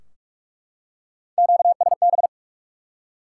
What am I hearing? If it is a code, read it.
Morse code, heard as a keyed tone, about 47 wpm, 700 Hz. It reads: QSB
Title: Top 90 QSO Elements 45wpm